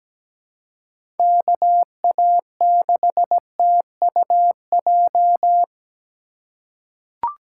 Morse code, heard as a keyed tone, about 17 wpm, 700 Hz. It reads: KA6TUJ
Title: USA Call Signs - 6 Characters - 2x Character Spacing - 17wpm